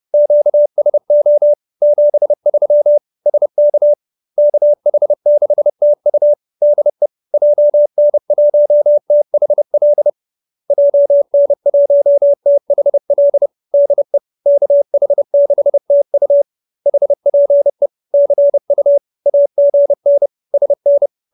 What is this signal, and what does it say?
Morse code, heard as a keyed tone, about 30 wpm, 590 Hz. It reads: QSO 73 SK KH6TU DE JN1THL JN1THL DE KH6TU HPE CU AGN SN